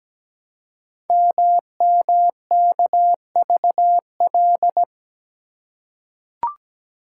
Morse code, heard as a keyed tone, about 17 wpm, 700 Hz. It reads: MMKVL
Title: Code Groups - 5 Letters - 17wpm